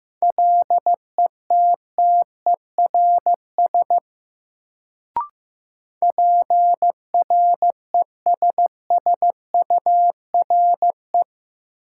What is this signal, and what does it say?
Morse code, heard as a keyed tone, about 15 wpm, 700 Hz. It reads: LETTERS PRESSURE
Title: Sentences from Top 1000 Words - Review Part 6 - New Word Drill - 15wpm